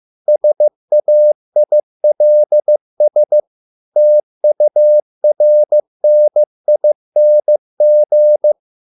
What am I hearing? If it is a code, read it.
Morse code, heard as a keyed tone, about 15 wpm, 600 Hz. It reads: SAILS TURNING